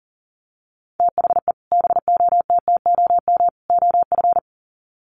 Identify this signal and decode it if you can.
Morse code, heard as a keyed tone, about 40 wpm, 700 Hz. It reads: THE BOTTOM OF
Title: Sets of 3 Words 40wpm